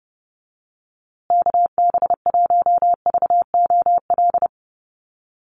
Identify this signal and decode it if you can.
Morse code, heard as a keyed tone, about 30 wpm, 700 Hz. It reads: KB1VOL
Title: USA Call Signs Repeated 3x 30wpm